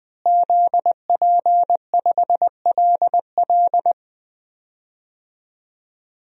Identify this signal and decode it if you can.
Morse code, heard as a keyed tone, about 21 wpm, 700 Hz. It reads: ZP5LL
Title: DXCC-100 Call Signs - 5 Characters - Repeated 3x - 20wpm